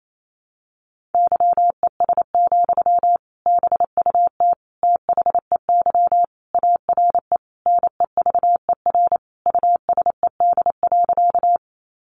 Morse code, heard as a keyed tone, about 28 wpm, 700 Hz. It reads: YES, BUT THEY ARE NEVER USED.
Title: Sentences from Top 600 Words 28wpm